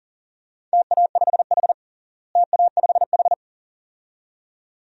Morse code, heard as a keyed tone, about 37 wpm, 700 Hz. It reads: TA5H TA5H
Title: DXCC-100 Call Signs - 4 Characters - Repeated 2x - 40wpm